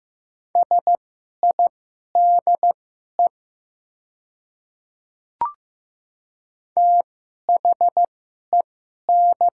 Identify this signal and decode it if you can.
Morse code, heard as a keyed tone, about 15 wpm, spaced as for 11 wpm, 700 Hz. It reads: SIDE THEN
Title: 2x Character Spacing - Top 100 Words - 15wpm